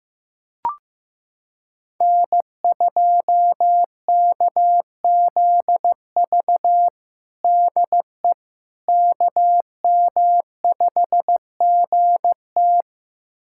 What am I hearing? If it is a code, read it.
Morse code, heard as a keyed tone, about 15 wpm, 700 Hz. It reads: N2KZV DE KM5GT
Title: USA Callsign DE Callsign - 5 Characters - 15wpm